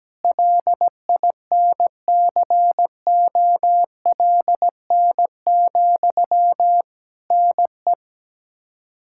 Morse code, heard as a keyed tone, about 17 wpm, 700 Hz. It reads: LINCOLN, NE
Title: Practice QSO - Exchange 1 - Beginning - 17wpm